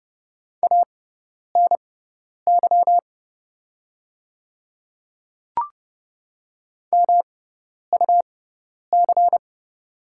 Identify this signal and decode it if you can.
Morse code, heard as a keyed tone, about 30 wpm, spaced as for 10 wpm, 690 Hz. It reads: ANY MUC
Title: 6x Character Spacing - Top 300 Words - 30wpm